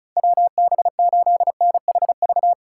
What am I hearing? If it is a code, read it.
Morse code, heard as a keyed tone, about 36 wpm, 700 Hz. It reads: WB8NHV
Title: USA Call Signs - 6 Characters - 3x Character Spacing - 35wpm